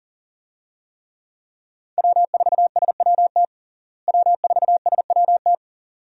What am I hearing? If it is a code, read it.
Morse code, heard as a keyed tone, about 37 wpm, 700 Hz. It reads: W4SWT W4SWT
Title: USA Call Signs - 5 Characters - Repeated 2x - 40wpm